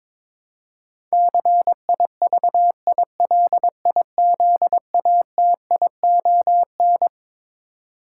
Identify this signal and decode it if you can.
Morse code, heard as a keyed tone, about 22 wpm, 700 Hz. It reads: CIVILIZATION
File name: Twelve Letter Words - 22wpm